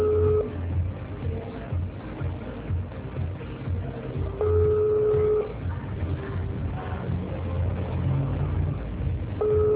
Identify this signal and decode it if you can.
Ringback tone (Europe, 425 Hz).